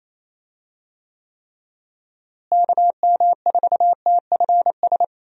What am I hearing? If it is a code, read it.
Morse code, heard as a keyed tone, about 27 wpm, 700 Hz. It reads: KM4TFS
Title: USA Call Signs - 6 Characters - Repeated 3x - 28wpm